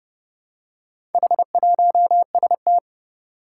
Morse code, heard as a keyed tone, about 30 wpm, 700 Hz. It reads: H1ST
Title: Four Challenging Letter-Number Combinations 30wpm